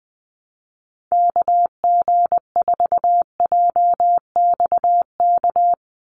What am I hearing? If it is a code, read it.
Morse code, heard as a keyed tone, about 20 wpm, 700 Hz. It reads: KG4JXK
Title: USA Call Signs 20wpm